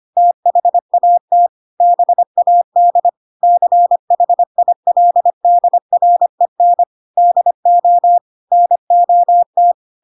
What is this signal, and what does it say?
Morse code, heard as a keyed tone, about 25 wpm, 700 Hz. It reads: THAT BAD CHILDREN DO NOT
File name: Pinocchio / Pinocchio ch4 25 25 ik1hge